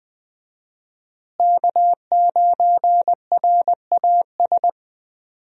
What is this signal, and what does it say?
Morse code, heard as a keyed tone, about 20 wpm, 700 Hz. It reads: K9RAS
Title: USA Call Signs - 5 Characters - Repeated 2x - 20wpm